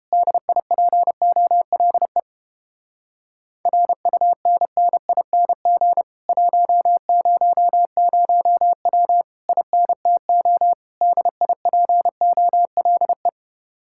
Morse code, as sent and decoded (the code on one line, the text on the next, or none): -.. .. .--. --- .-.. / .-. ..- -. -. .. -. --. / .---- ----- ----- .-- / .. -. - --- / -.. .. .--. --- .-.. .
DIPOLE RUNNING 100W INTO DIPOLE